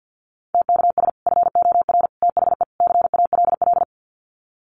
Morse code, heard as a keyed tone, about 50 wpm, 700 Hz. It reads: TKS FOR THE CALL
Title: Common QSO Phrases 50wpm